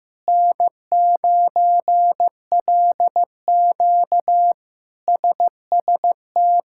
Morse code, heard as a keyed tone, about 15 wpm, 700 Hz. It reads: N9LQ SST
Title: K1USN SST QSO Practice - 15wpm